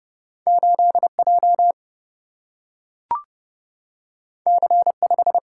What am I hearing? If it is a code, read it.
Morse code, heard as a keyed tone, about 30 wpm, 700 Hz. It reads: 8J C5
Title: Single-Letter Single-Number 30wpm